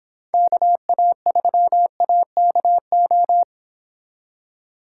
Morse code, heard as a keyed tone, about 26 wpm, 700 Hz. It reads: KA3AKO